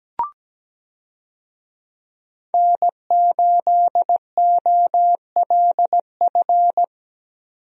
Morse code, heard as a keyed tone, about 17 wpm, 700 Hz. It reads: N8OLF
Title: USA Call Signs - 5 Characters - 17wpm